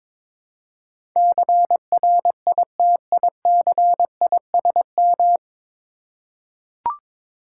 Morse code, heard as a keyed tone, about 22 wpm, 700 Hz. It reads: CRITICISM
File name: practice / 22wpm / 9 Letter Words - First 3 Letters Given - 22wpm